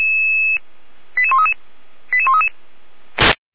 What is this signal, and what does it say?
Five-tone selective call (ZVEI), heard as quick runs of five tones, tones of about 70 ms. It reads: 88133 88133
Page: SDR/ZVEI